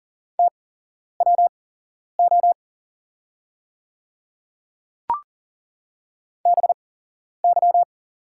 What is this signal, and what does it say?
Morse code, heard as a keyed tone, about 40 wpm, spaced as for 11 wpm, 700 Hz. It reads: TWO BY
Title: 8x Character Spacing - Top 100 Words - 40wpm